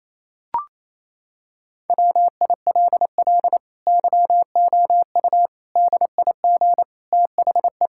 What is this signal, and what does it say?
Morse code, heard as a keyed tone, about 28 wpm, 700 Hz. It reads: WILL YOU DIG THE
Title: Sentences from Top 1600 Words 28wpm